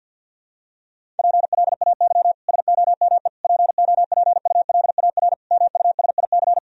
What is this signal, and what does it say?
Morse code, heard as a keyed tone, about 50 wpm, 700 Hz. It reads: PLAY SOME POPULAR MUSIC
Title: Sentences from Top 1000 Words - Review Part 3 - 50wpm